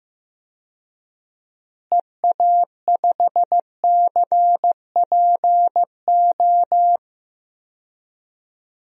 Morse code, heard as a keyed tone, about 15 wpm, 700 Hz. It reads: EA5CPO
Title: DXCC-100 Call Signs - 6 Characters - Repeated 2x - 15wpm